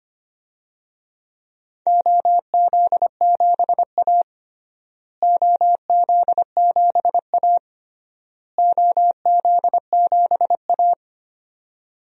Morse code, heard as a keyed tone, about 25 wpm, 700 Hz. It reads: OZ7A OZ7A OZ7A
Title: DXCC-100 Call Signs - 4 Characters - Repeated 3x - 25wpm